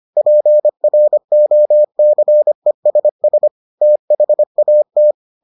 Morse code, heard as a keyed tone, about 25 wpm, 600 Hz. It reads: PROCESS THAT